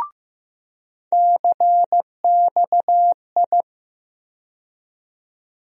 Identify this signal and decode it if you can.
Morse code, heard as a keyed tone, about 15 wpm, 700 Hz. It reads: CXI
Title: USA Call Sign Postfixes - 3 Letters 15wpm